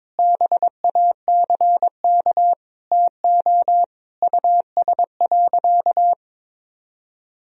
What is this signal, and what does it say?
Morse code, heard as a keyed tone, about 22 wpm, 700 Hz. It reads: BACK TO US.
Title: Sentences from Top 1000 Words - Review Part 3 - 22wpm